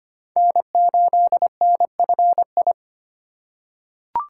Morse code, heard as a keyed tone, about 25 wpm, 700 Hz. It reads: N8NFI
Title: USA Call Signs - 5 Characters - 2x Character Spacing - 25wpm